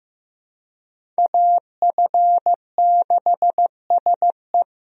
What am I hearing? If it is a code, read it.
Morse code, heard as a keyed tone, about 15 wpm, 700 Hz. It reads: AF6SE
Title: USA Call Signs - 5 Characters - Repeated 3x - 15wpm